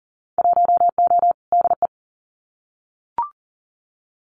Morse code, heard as a keyed tone, about 40 wpm, 700 Hz. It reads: 1O DE
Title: Winter Field Day Exchanges 40wpm